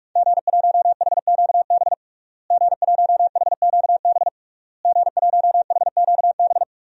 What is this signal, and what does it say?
Morse code, heard as a keyed tone, about 46 wpm, 700 Hz. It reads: G1HQB G1HQB G1HQB
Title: DXCC-100 Call Signs - 5 Characters - Repeated 3x - 45wpm